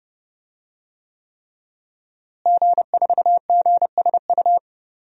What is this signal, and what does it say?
Morse code, heard as a keyed tone, about 30 wpm, 700 Hz. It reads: G4GSU
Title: DXCC-100 Call Signs - 5 Characters - Repeated 3x - 30wpm